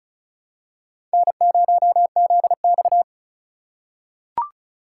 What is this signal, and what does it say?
Morse code, heard as a keyed tone, about 36 wpm, 700 Hz. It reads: N0ZX
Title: USA Call Signs - 4 Characters - 2x Character Spacing - 35wpm